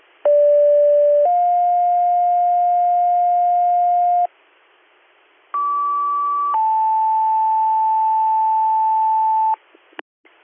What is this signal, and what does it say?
Two-tone page, heard as a one-second tone, then a three-second tone, about 580 then 710 Hz, about 1150 then 880 Hz. It is heard from 0.3 to 9.6 s.